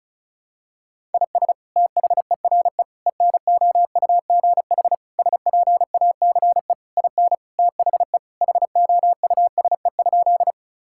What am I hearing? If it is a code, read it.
Morse code, heard as a keyed tone, about 36 wpm, 700 Hz. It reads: IS THERE ENOUGH SPACE IN THE HOUSE?
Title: Sentences from Top 1000 Words - Review Part 5 - 35wpm